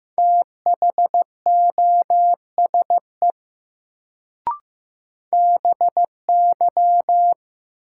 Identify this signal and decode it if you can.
Morse code, heard as a keyed tone, about 15 wpm, 700 Hz. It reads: THOSE BY
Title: Sets of 2 Words - New Word Drill - 15wpm